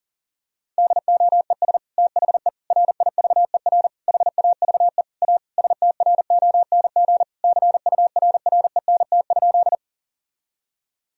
Morse code, heard as a keyed tone, about 40 wpm, 700 Hz. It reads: DOES THE RIVER HAVE A STRONG CURRENT?
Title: Sentences from Top 1000 Words - Review Part 4 - 40wpm